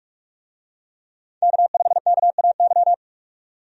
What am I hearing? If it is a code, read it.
Morse code, heard as a keyed tone, about 46 wpm, 700 Hz. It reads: K5KAY
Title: USA Call Signs - 5 Characters - 45wpm